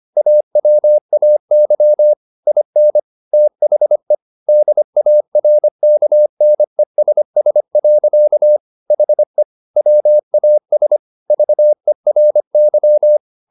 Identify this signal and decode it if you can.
Morse code, heard as a keyed tone, about 25 wpm, 600 Hz. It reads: AWAY IN THE DARKNESS. HE WAS VERY
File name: the old man and the sea part 1 25wpm